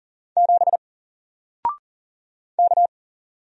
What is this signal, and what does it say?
Morse code, heard as a keyed tone, about 40 wpm, 700 Hz. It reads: 7 K
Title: Single Letter-Number Alt 40wpm